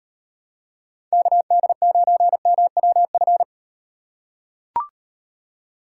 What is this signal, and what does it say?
Morse code, heard as a keyed tone, about 38 wpm, 700 Hz. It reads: KD9MWF